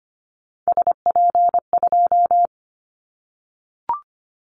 Morse code, heard as a keyed tone, about 25 wpm, 700 Hz. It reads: SP2
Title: Callsign Prefixes for Top 25 DXCC - Repeated 3x - 25wpm